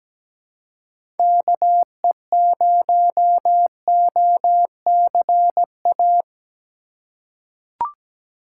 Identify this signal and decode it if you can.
Morse code, heard as a keyed tone, about 17 wpm, 700 Hz. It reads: KE0OCA
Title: USA Call Signs - 2x Character Spacing - 17wpm